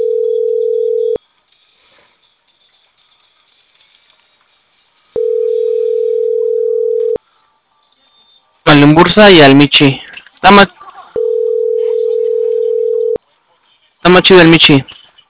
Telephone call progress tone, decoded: ringback tone

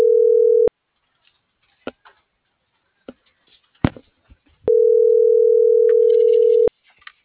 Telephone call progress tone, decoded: ringback tone

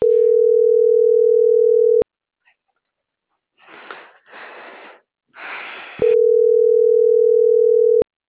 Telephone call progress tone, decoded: ringback tone

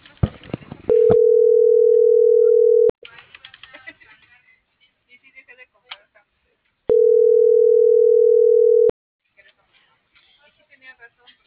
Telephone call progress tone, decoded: ringback tone

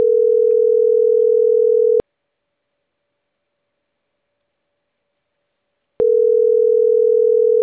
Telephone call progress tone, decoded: ringback tone

